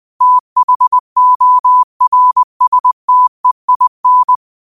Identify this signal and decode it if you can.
Morse code, heard as a keyed tone, about 20 wpm, 1000 Hz. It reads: THORSTEIN